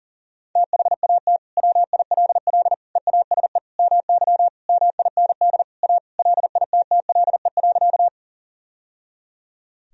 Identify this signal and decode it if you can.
Morse code, heard as a keyed tone, about 37 wpm, 700 Hz. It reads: THAT WILL EASE MY MIND A LITTLE.
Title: Sentences from Top 1000 Words - Review Part 1 - 40wpm